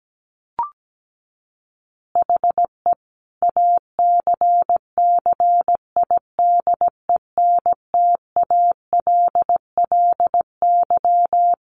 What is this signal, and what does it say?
Morse code, heard as a keyed tone, about 17 wpm, 700 Hz. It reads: HE ACCIDENTALLY
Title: Additional Easy Phrases 17wpm